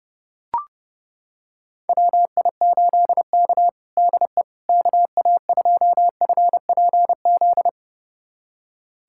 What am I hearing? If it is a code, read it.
Morse code, heard as a keyed tone, about 30 wpm, 700 Hz. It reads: WI8K DE KA2FPZ
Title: USA Callsign DE Callsign 30wpm